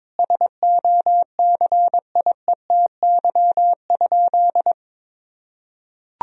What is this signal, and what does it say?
Morse code, heard as a keyed tone, about 22 wpm, 700 Hz. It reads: SOCIETY?
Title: Sentences from Top 1000 Words - Review Part 4 - 22wpm